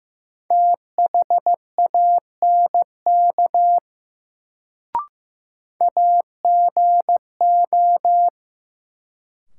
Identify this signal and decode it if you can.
Morse code, heard as a keyed tone, about 15 wpm, 700 Hz. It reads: THANK AGO
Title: Sets of 2 Words - New Word Drill - 15wpm